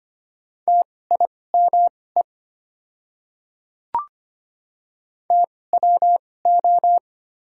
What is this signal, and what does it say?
Morse code, heard as a keyed tone, about 25 wpm, spaced as for 18 wpm, 700 Hz. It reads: TIME TWO
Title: 2x Character Spacing - Top 100 Words - 25wpm